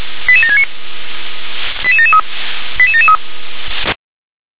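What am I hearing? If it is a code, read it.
Five-tone selective call (ZVEI), heard as quick runs of five tones, tones of about 70 ms. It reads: 88769 88773 88773
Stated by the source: ZVEI five-tone call received on an SDR.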